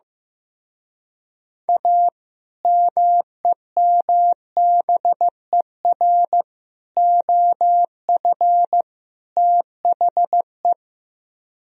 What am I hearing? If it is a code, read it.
Morse code, heard as a keyed tone, about 15 wpm, 700 Hz. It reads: A MEMBER OF THE
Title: Speed-Racing - Sets of 4 Words 15wpm